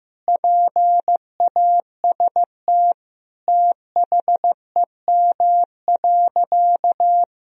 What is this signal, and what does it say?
Morse code, heard as a keyed tone, about 15 wpm, 700 Hz. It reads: PAST THEM.
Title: Sentences from Top 1000 Words - Review Part 3 - 15wpm